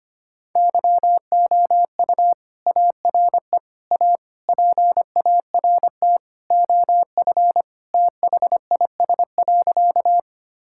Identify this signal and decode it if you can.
Morse code, heard as a keyed tone, about 25 wpm, 700 Hz. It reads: YOU ARE A PART OF THIS.
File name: practice / 25wpm / Sentences with 3-Word Phrases - 25wpm